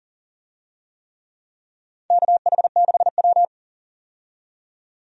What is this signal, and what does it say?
Morse code, heard as a keyed tone, about 38 wpm, 700 Hz. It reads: KH6W
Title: DXCC-100 Call Signs - 4 Characters - 40wpm